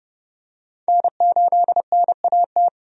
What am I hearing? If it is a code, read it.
Morse code, heard as a keyed tone, about 30 wpm, 700 Hz. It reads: N8NAT
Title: USA Call Signs - 5 Characters - 30wpm